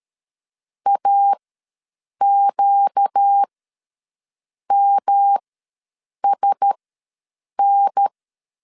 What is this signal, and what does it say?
Morse code, heard as a keyed tone, about 13 wpm, spaced as for 7 wpm, 800 Hz. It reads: AQMSN
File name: LEZIONI / 6 PKX0 / 6 PK ws 12wpm 3116